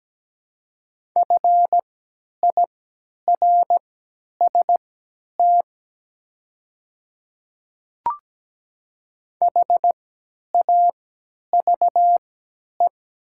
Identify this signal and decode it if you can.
Morse code, heard as a keyed tone, about 17 wpm, spaced as for 10 wpm, 700 Hz. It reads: FIRST HAVE
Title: 3x Character Spacing - Top 200 Words - 17wpm